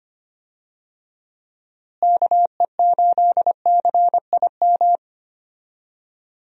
Morse code, heard as a keyed tone, about 25 wpm, 700 Hz. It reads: KE8CIM
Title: USA Call Signs - 6 Characters - Repeated 3x - 25wpm